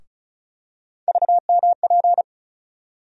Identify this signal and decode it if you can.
Morse code, heard as a keyed tone, about 36 wpm, 700 Hz. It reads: VMP
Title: USA Call Sign Postfixes - 3 Letters 35wpm